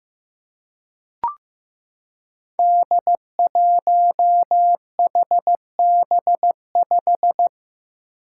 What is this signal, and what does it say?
Morse code, heard as a keyed tone, about 15 wpm, 700 Hz. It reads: D1HB5
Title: Five Challenging Letter-Number Combinations 15wpm